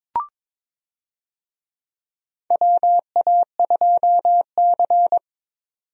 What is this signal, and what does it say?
Morse code, heard as a keyed tone, about 22 wpm, 700 Hz. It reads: WA2C
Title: USA Call Signs - 4 Characters - 22wpm